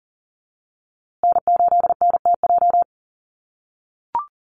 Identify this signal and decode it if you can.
Morse code, heard as a keyed tone, about 40 wpm, 700 Hz. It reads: N8NTJ
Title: USA Call Signs Repeated 3x 40wpm